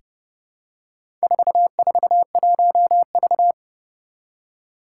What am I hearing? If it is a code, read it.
Morse code, heard as a keyed tone, about 30 wpm, 700 Hz. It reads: 441V
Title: Four Challenging Letter-Number Combinations 30wpm